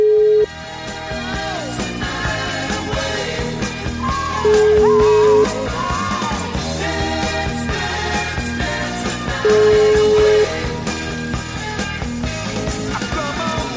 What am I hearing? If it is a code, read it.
Ringback tone (Europe, 425 Hz), with noise.